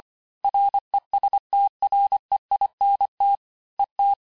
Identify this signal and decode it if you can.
Morse code, heard as a keyed tone, about 25 wpm, 800 Hz. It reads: RESTREINT ET
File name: cours-du-lundi files / cours-1595-teste-CLAIR-vitesse18-MPM-puis23MPM